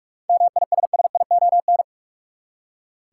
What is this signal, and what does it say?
Morse code, heard as a keyed tone, about 46 wpm, 700 Hz. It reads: MISSION
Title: Seven Letter Words - 45wpm